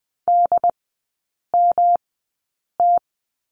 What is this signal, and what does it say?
Morse code, heard as a keyed tone, about 20 wpm, spaced as for 8 wpm, 700 Hz. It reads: DMT
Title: 145 Lesson 24e - Letter y - Three Characters 20wpm